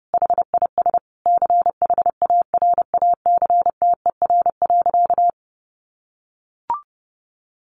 Morse code, heard as a keyed tone, about 30 wpm, 700 Hz. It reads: HIS CHARACTER.
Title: Sentences from Top 1200 Words 30wpm